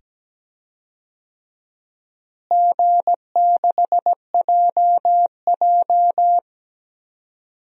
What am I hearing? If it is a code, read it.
Morse code, heard as a keyed tone, about 17 wpm, 700 Hz. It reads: G6JJ
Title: DXCC-100 Call Signs - 4 Characters - Repeated 2x - 17wpm